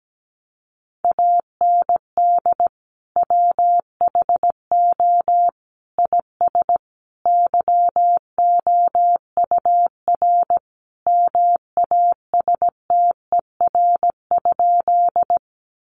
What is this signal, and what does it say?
Morse code, heard as a keyed tone, about 17 wpm, 700 Hz. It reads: AND WHO IS YOUR MASTER?
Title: Sentences from Top 1300 Words 17wpm